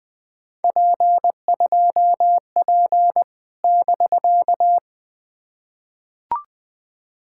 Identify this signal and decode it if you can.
Morse code, heard as a keyed tone, about 20 wpm, 700 Hz. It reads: P2P <BK>
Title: POTA QSO Practice - 20wpm